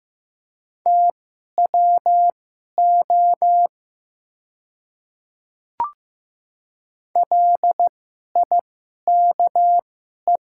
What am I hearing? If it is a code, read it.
Morse code, heard as a keyed tone, about 15 wpm, spaced as for 11 wpm, 700 Hz. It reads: TWO LIKE
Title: 2x Character Spacing - Top 100 Words - 15wpm